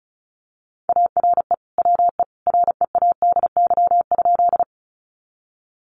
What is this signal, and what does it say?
Morse code, heard as a keyed tone, about 34 wpm, 700 Hz. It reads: ARE WE READY?
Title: Sentences from Top 400 Words 35wpm